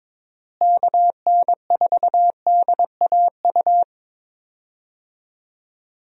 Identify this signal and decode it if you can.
Morse code, heard as a keyed tone, about 21 wpm, 700 Hz. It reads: KN4DAU
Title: USA Call Signs - 6 Characters - Repeated 2x - 22wpm